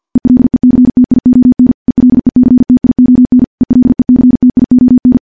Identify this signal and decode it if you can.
Morse code, heard as a keyed tone, about 46 wpm, 250 Hz. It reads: ELECTION ELECTION ELECTION
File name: morse / morse-training-50WPM